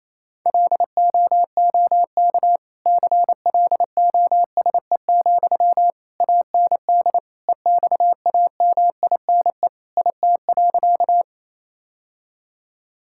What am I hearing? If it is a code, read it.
Morse code, heard as a keyed tone, about 28 wpm, 700 Hz. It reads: LOOK CLOSE, AND EXAMINE IT.
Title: Sentences from Top 1600 Words 28wpm